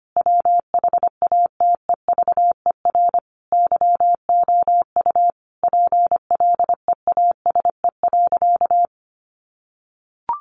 Morse code, heard as a keyed tone, about 25 wpm, 700 Hz. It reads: WHATEVER YOU PLEASE.
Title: Sentences from Top 1500 Words 25wpm